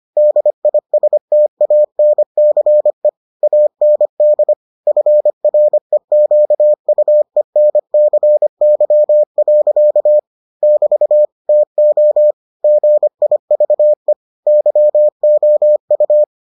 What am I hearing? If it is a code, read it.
Morse code, heard as a keyed tone, about 25 wpm, 600 Hz. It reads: DISTANCE AND FREQUENCY. <BT> TO GIVE YOU